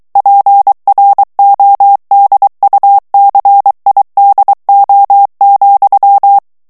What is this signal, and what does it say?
Morse code, heard as a keyed tone, about 23 wpm, 800 Hz. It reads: PRODUCIDO,
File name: morse mp3 / ESPAGNOL TEXTE EN CLAIR 2 VITESSE 22 MOTS MINUTES